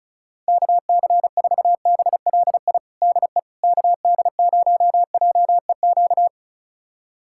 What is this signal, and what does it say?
Morse code, heard as a keyed tone, about 36 wpm, 700 Hz. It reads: KC4BLI DE KD0JEQ
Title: USA Callsign DE Callsign 35wpm